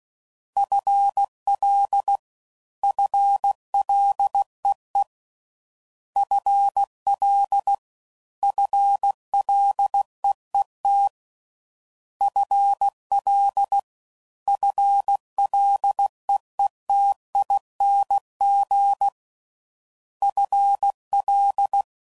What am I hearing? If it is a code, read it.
Morse code, heard as a keyed tone, about 16 wpm, 800 Hz. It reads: FL FLEE FL FLEET FL FLEETING FL
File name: Prefixes / FLs & THs / FL Prefixes 1